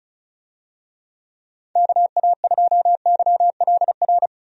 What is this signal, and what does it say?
Morse code, heard as a keyed tone, about 36 wpm, 700 Hz. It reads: KA2YLR
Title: USA Call Signs - 6 Characters - Repeated 3x - 35wpm